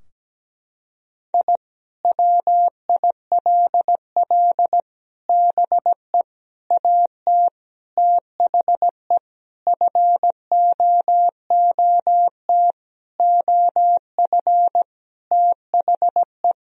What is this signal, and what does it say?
Morse code, heard as a keyed tone, about 17 wpm, 700 Hz. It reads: I WILL BE AT THE FOOT OF THE
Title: Sentences from Top 1000 Words - Review Part 3 - 17wpm